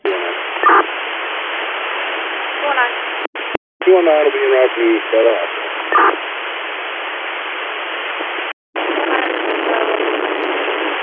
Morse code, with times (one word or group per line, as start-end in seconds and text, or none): none